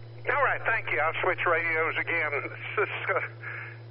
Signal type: Emergency Alert System (SAME) tones